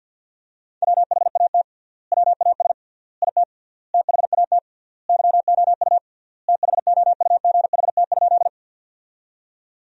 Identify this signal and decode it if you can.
Morse code, heard as a keyed tone, about 50 wpm, 700 Hz. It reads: WHAT WAS IT THAT YOU THOUGHT?
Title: 3x Word-Spacing - Sentences from Top 300 Words - 50wpm